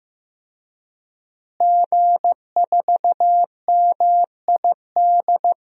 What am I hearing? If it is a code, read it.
Morse code, heard as a keyed tone, about 15 wpm, 700 Hz. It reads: G4MID
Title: DXCC-100 Call Signs - 5 Characters - Repeated 3x - 15wpm